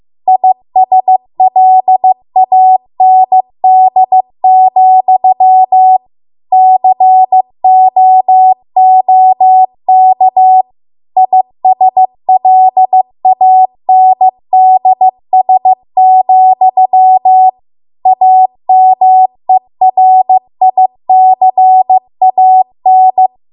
Morse code, heard as a keyed tone, about 15 wpm, 750 Hz. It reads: ISLAND, COOK ISLANDS, AMERICAN